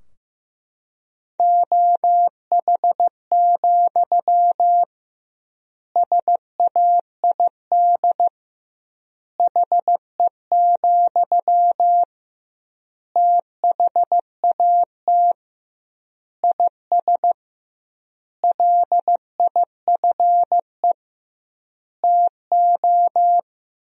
Morse code, heard as a keyed tone, about 15 wpm, 700 Hz. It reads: OH, SAID HE, THAT IS LIFE TO
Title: 2x Word-Spacing - Sentences from Top 500 Words - 15wpm